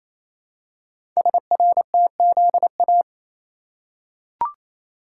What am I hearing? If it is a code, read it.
Morse code, heard as a keyed tone, about 27 wpm, 700 Hz. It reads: SRTZA
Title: Code Groups - 5 Letters - 28wpm